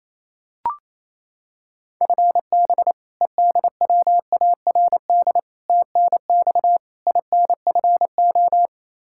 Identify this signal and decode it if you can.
Morse code, heard as a keyed tone, about 28 wpm, 700 Hz. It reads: FB EDWARD TNX INFO
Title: Practice QSO - Exchange 3 - More Info - 28wpm